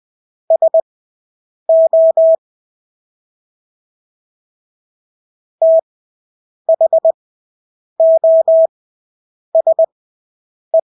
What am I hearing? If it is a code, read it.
Morse code, heard as a keyed tone, about 20 wpm, spaced as for 8 wpm, 650 Hz. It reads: SO THOSE
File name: Engelske ord 20/8 wpm